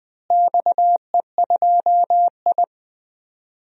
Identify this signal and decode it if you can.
Morse code, heard as a keyed tone, about 20 wpm, 700 Hz. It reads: XE2I
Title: DXCC-100 Call Signs - 4 Characters - Repeated 2x - 20wpm